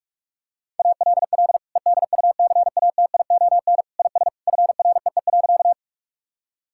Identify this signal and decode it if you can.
Morse code, heard as a keyed tone, about 46 wpm, 700 Hz. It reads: ALL EDUCATION IS FREE.
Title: Sentences from Top 1000 Words - Review Part 6 - 45wpm